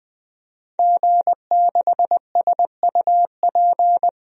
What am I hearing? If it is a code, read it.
Morse code, heard as a keyed tone, about 21 wpm, 700 Hz. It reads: G6SUP
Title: DXCC-100 Call Signs - 5 Characters - 20wpm